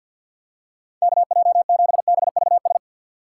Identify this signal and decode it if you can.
Morse code, heard as a keyed tone, about 52 wpm, 690 Hz. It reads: KJ7BVS
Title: USA Call Signs - 2x Character Spacing - 50wpm